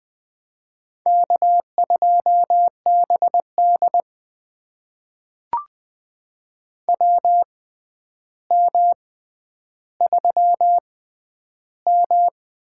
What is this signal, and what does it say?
Morse code, heard as a keyed tone, about 21 wpm, spaced as for 10 wpm, 700 Hz. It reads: K2BD WM3M
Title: USA Call Signs - 4 Characters - 6x Character Spacing - 20wpm